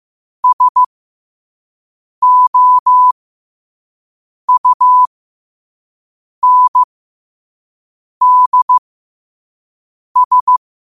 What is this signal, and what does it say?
Morse code, heard as a keyed tone, about 15 wpm, spaced as for 5 wpm, 1000 Hz. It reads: SOUNDS
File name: qotdfiles / 5wpm